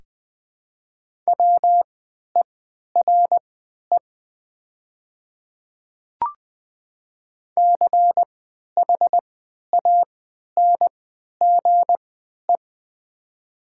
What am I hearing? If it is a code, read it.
Morse code, heard as a keyed tone, about 20 wpm, spaced as for 11 wpm, 700 Hz. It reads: WERE CHANGE
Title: 3x Character Spacing - Top 500 Words - 20wpm